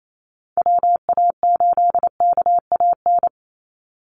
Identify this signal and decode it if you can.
Morse code, heard as a keyed tone, about 28 wpm, 700 Hz. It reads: WA8KAN
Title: Call Signs Repeated 3x - 28wpm